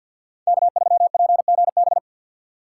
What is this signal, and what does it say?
Morse code, heard as a keyed tone, about 53 wpm, 700 Hz. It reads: K3PGB
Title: USA Call Signs - 5 Characters - 3x Character Spacing - 50wpm